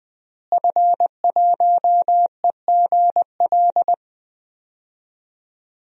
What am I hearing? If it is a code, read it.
Morse code, heard as a keyed tone, about 20 wpm, 700 Hz. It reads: F1EGL
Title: DXCC-100 Call Signs - 5 Characters - Repeated 2x - 20wpm